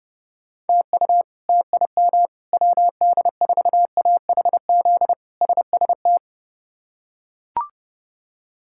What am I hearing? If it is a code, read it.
Morse code, heard as a keyed tone, about 30 wpm, 700 Hz. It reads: TU TIM WD4AHZ SST
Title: K1USN SST QSO Practice - 30wpm